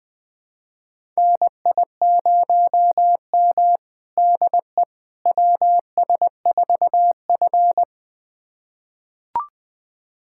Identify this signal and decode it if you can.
Morse code, heard as a keyed tone, about 21 wpm, 700 Hz. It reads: NI0M DE WS4F
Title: USA Callsign DE Callsign - 4 Characters - 20wpm